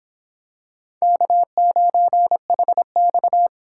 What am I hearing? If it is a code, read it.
Morse code, heard as a keyed tone, about 25 wpm, 700 Hz. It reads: K9HX